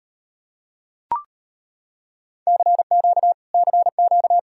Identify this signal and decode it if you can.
Morse code, heard as a keyed tone, about 38 wpm, 700 Hz. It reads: CQ CQ